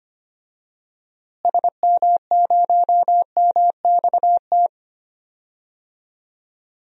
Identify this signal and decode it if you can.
Morse code, heard as a keyed tone, about 25 wpm, 700 Hz. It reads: SM0MXT
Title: DXCC-100 Call Signs - 6 Characters - Repeated 3x - 25wpm